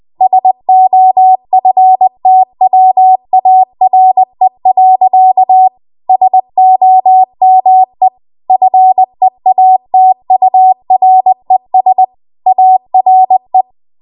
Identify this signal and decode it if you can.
Morse code, heard as a keyed tone, about 20 wpm, 750 Hz. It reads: SOFTWARE. SOME FEATURES ARE
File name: Archive / 20 WPM / 20WPM